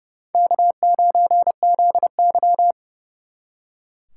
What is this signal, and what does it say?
Morse code, heard as a keyed tone, about 38 wpm, 700 Hz. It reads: K9ZY